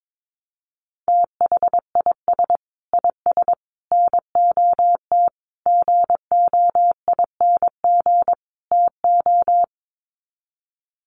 Morse code, heard as a keyed tone, about 22 wpm, 700 Hz. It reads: THIS IS NOT GOING TO
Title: Sets of 5 Words Encore2 22wpm